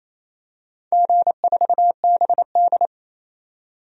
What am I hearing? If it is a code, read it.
Morse code, heard as a keyed tone, about 27 wpm, 700 Hz. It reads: G4BD
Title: DXCC-100 Call Signs - 4 Characters - 28wpm